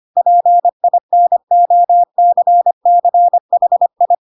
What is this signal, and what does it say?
Morse code, heard as a keyed tone, about 25 wpm, 700 Hz. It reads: PINOCCHI
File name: Pinocchio / Pinocchio ch3 25 25 ik1hge